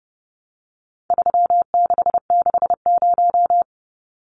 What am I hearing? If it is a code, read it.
Morse code, heard as a keyed tone, about 30 wpm, 700 Hz. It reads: 3660
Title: Numbers - 4-Digit 30wpm